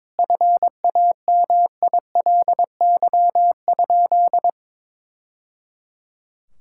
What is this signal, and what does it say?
Morse code, heard as a keyed tone, about 22 wpm, 700 Hz. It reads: FAMILY?
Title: Sentences from Top 1000 Words - Review Part 6 - 22wpm